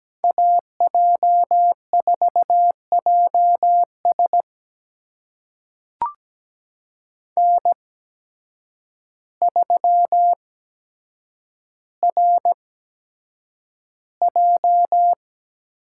Morse code, heard as a keyed tone, about 17 wpm, spaced as for 8 wpm, 700 Hz. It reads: AJ4JS N3RJ
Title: USA Call Signs - 5 Characters - 8x Character Spacing - 17wpm